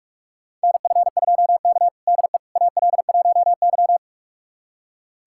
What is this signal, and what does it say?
Morse code, heard as a keyed tone, about 42 wpm, 700 Hz. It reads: NF2K DE AL1Y
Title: USA Callsign DE Callsign - 4 Characters - 45wpm